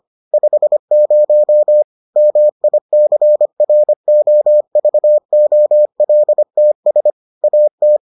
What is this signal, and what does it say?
Morse code, heard as a keyed tone, about 25 wpm, 600 Hz. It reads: 50 MICROVOLTS AT